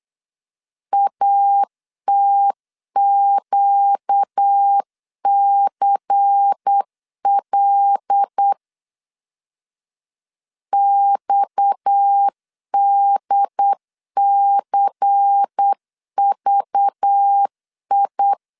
Morse code, heard as a keyed tone, about 8 wpm, 800 Hz. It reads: ATQCL XDCVI
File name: CW / DET / 1 6 1500C S C 8wpm 5219